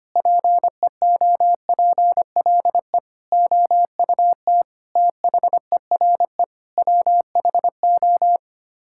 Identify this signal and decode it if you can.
Morse code, heard as a keyed tone, about 25 wpm, 700 Hz. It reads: PEOPLE OUT THERE WHO
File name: Sets of 9 Words - 25wpm